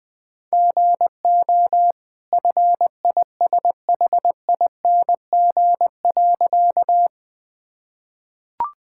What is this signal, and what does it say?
Morse code, heard as a keyed tone, about 20 wpm, 700 Hz. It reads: GO FISHING.
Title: Easy Sentences with Hints - 20wpm